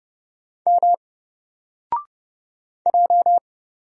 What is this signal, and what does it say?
Morse code, heard as a keyed tone, about 31 wpm, 700 Hz. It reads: M J
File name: Single-Letter-Number-30wpm